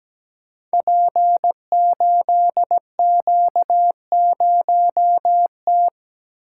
Morse code, heard as a keyed tone, about 17 wpm, 700 Hz. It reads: P8Q0T
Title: Code Groups - 5 Letter:Numbers - 17wpm